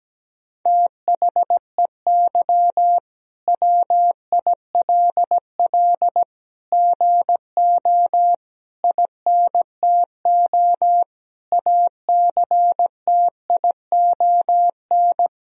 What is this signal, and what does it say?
Morse code, heard as a keyed tone, about 17 wpm, 700 Hz. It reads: THEY WILL GO INTO ACTION
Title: Sentences from Top 1000 Words - Review Part 4 - 17wpm